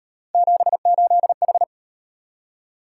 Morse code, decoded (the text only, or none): 78H